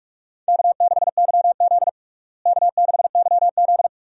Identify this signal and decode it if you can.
Morse code, heard as a keyed tone, about 46 wpm, 700 Hz. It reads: K6YZ K6YZ
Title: USA Call Signs - 4 Characters - Repeated 3x - 45wpm